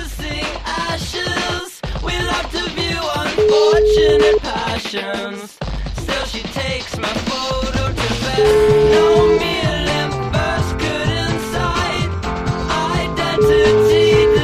Ringback tone (Europe, 425 Hz), with noise.